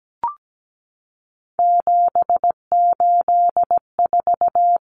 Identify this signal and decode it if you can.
Morse code, heard as a keyed tone, about 17 wpm, 700 Hz. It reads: 784
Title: Numbers - 3-Digit 17wpm